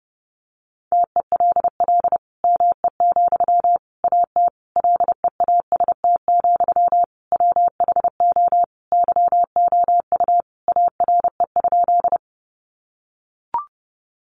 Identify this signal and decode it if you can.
Morse code, heard as a keyed tone, about 30 wpm, 700 Hz. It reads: TELL ME, AT LEAST, WHO YOU ARE?
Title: Sentences from Top 700 Words 30wpm